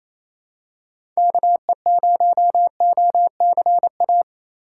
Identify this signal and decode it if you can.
Morse code, heard as a keyed tone, about 28 wpm, 700 Hz. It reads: KE0OCA
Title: USA Call Signs - 6 Characters - 2x Character Spacing - 28wpm